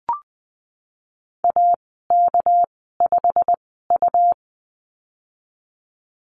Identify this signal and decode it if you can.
Morse code, heard as a keyed tone, about 20 wpm, spaced as for 14 wpm, 700 Hz. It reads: AK5U
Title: 200 Lesson 32g - Letter k - Call Signs 20wpm